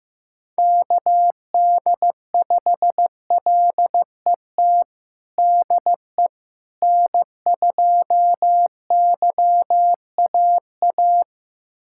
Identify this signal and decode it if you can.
Morse code, heard as a keyed tone, about 15 wpm, 700 Hz. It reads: KD5LET DE N2YAA
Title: USA Callsign DE Callsign 15wpm